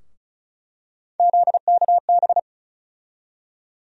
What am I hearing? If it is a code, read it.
Morse code, heard as a keyed tone, about 36 wpm, 700 Hz. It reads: ZKB
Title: USA Call Sign Postfixes - 3 Letters 35wpm